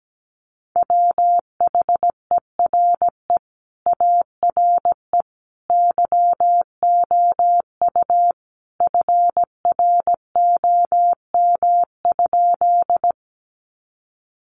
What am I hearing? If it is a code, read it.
Morse code, heard as a keyed tone, about 17 wpm, 700 Hz. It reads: WHERE ARE YOU FROM?
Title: Sentences from Top 200 Words 17wpm